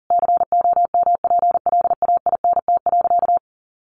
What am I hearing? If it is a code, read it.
Morse code, heard as a keyed tone, about 40 wpm, 700 Hz. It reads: COMPLAINT.
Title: Sentences from Top 1400 Words 40wpm